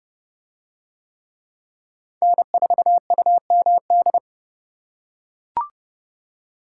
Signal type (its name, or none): Morse code